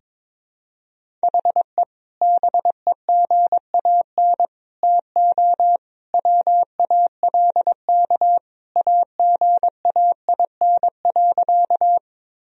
Morse code, heard as a keyed tone, about 21 wpm, 700 Hz. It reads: HE BEGAN TO WALK AGAIN.
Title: Sentences from Top 1000 Words - Review Part 1 - 22wpm